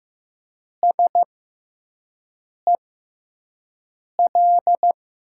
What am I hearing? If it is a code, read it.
Morse code, heard as a keyed tone, about 15 wpm, spaced as for 5 wpm, 700 Hz. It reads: SEL
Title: USA Call Signs - 6x Character Spacing - 15wpm